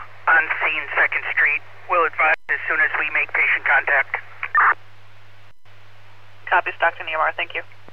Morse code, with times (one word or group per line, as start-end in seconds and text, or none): none